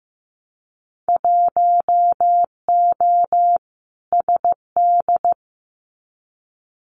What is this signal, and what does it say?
Morse code, heard as a keyed tone, about 15 wpm, 700 Hz. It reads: 1O SD
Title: Winter Field Day Exchanges Encore 15wpm